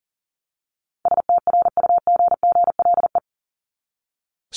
Morse code, heard as a keyed tone, about 40 wpm, 700 Hz. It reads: STRUGGLE
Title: Top 1751-2000 Words 40wpm